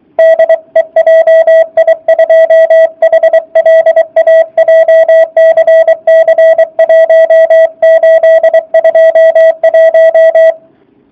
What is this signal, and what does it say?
Morse code, heard as a keyed tone, about 23 wpm, 660 Hz. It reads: DE JI2HLA JCC 1821